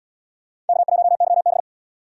Morse code, heard as a keyed tone, about 69 wpm, 700 Hz. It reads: K1YB